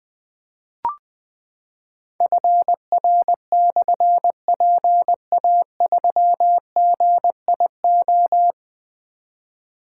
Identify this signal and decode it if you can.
Morse code, heard as a keyed tone, about 20 wpm, 700 Hz. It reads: FR/PA3GIO
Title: Difficult Call Signs 20wpm